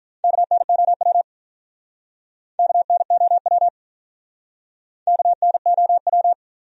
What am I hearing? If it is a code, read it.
Morse code, heard as a keyed tone, about 47 wpm, 700 Hz. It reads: KNOW KNOW KNOW